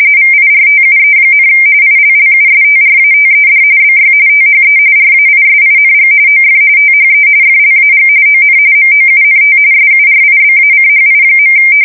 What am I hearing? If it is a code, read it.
RTTY, about 45.45 baud. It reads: VIRTUE, ILLUSTRATED BY THE RAISING OF THE FLAG ON MOUNT SURIBACHI.
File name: fsk170 / battleshipnj